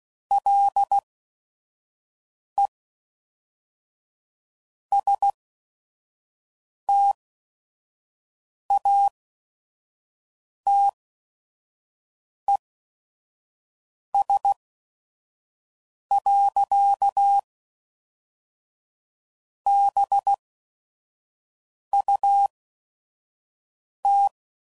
Morse code, heard as a keyed tone, about 16 wpm, spaced as for 5 wpm, 800 Hz. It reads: LE STATES. BUT
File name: CW Code Groups @5WPM / Technical Subjects / AND Gate